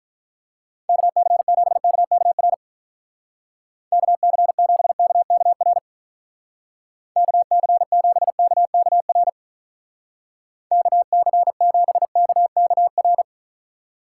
Morse code, heard as a keyed tone, about 44 wpm, 700 Hz. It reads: KC7KKR KC7KKR KC7KKR KC7KKR